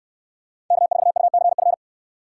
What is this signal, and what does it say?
Morse code, heard as a keyed tone, about 69 wpm, 700 Hz. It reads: K4UCF